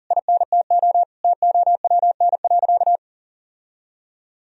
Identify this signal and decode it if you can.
Morse code, heard as a keyed tone, about 37 wpm, 700 Hz. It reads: INTO TOWN.
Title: Sentences from Top 1000 Words - Review Part 2 - 40wpm